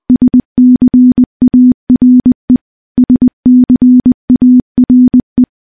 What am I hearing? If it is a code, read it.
Morse code, heard as a keyed tone, about 20 wpm, 250 Hz. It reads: SCARE SCARE
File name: morse-training-20WPM